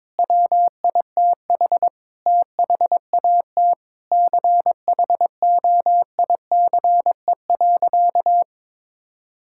Morse code, heard as a keyed tone, about 21 wpm, 700 Hz. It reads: WITH THAT CHOICE.
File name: Sentences with 7-Word Phrases - 22wpm